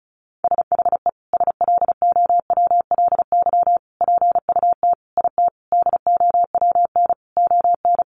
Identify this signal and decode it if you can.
Morse code, heard as a keyed tone, about 34 wpm, 700 Hz. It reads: SHE SLOWLY PUT IT DOWN ON
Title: Sentences from Top 1100 Words 35wpm